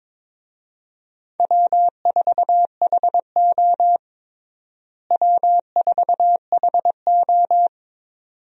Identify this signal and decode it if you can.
Morse code, heard as a keyed tone, about 21 wpm, 700 Hz. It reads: W4HO W4HO
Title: USA Call Signs - 4 Characters - Repeated 2x - 22wpm